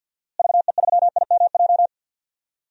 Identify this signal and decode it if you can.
Morse code, heard as a keyed tone, about 49 wpm, 700 Hz. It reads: VE3IMJ
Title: DXCC-100 Call Signs - 6 Characters - Repeated 2x - 50wpm